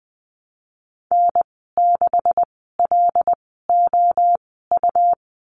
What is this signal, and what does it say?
Morse code, heard as a keyed tone, about 20 wpm, spaced as for 15 wpm, 700 Hz. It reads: N6LOU
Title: 161 Lesson 26g - Letter g - Call Signs 20wpm